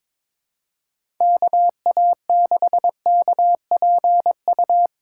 Morse code, heard as a keyed tone, about 22 wpm, 700 Hz. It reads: KA6KPU
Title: USA Call Signs - 2x Character Spacing - 22wpm